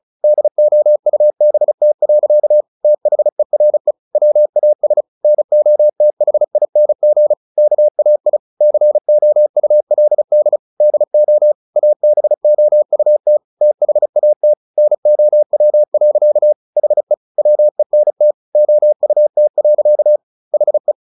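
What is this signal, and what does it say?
Morse code, heard as a keyed tone, about 34 wpm, 600 Hz. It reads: DOUBT. THERE WAS NOTHING KAI COULD DO ABOUT THAT NOW. HE WENT OUT. HE
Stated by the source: Morse practice at 35 wpm from CW med Gnister, sending Ken Follett's novel Never in English.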